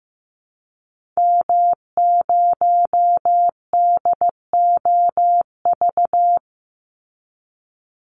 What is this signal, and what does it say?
Morse code, heard as a keyed tone, about 15 wpm, 700 Hz. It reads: M0DOV
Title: Call Signs Repeated 2x - 15wpm